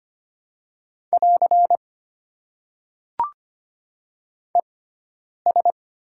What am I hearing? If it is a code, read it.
Morse code, heard as a keyed tone, about 25 wpm, spaced as for 9 wpm, 700 Hz. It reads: <AR> ES
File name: Top 90 QSO Elements - 6x Character Spacing - 25wpm